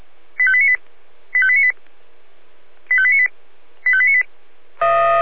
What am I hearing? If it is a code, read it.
Five-tone selective call (ZVEI), heard as quick runs of five tones, tones of about 70 ms. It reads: 85898 85898 85898 85898